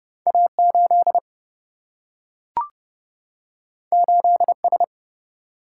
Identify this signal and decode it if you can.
Morse code, heard as a keyed tone, about 30 wpm, 700 Hz. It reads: A8 8S
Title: Single-Letter Single-Number 30wpm